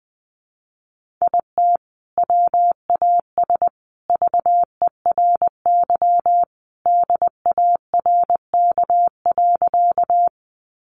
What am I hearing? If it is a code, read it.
Morse code, heard as a keyed tone, about 20 wpm, 700 Hz. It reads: IT WAS VERY DARK.
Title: Sentences from Top 500 Words 20wpm